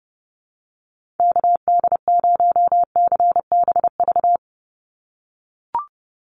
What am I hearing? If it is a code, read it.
Morse code, heard as a keyed tone, about 31 wpm, 700 Hz. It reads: KD0CBV
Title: USA Call Signs 30wpm